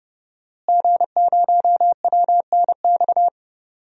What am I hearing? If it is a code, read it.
Morse code, heard as a keyed tone, about 32 wpm, 700 Hz. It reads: G0WNX
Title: DXCC-100 Call Signs - 5 Characters - Repeated 3x - 30wpm